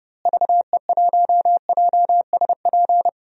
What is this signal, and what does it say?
Morse code, heard as a keyed tone, about 30 wpm, 700 Hz. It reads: VE1JSP
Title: DXCC-100 Call Signs - 6 Characters - Repeated 2x - 30wpm